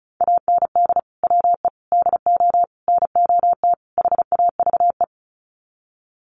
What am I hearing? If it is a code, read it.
Morse code, heard as a keyed tone, about 34 wpm, 700 Hz. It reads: AND WE DO NOT HAVE
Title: Sets of 5 Words Encore 35wpm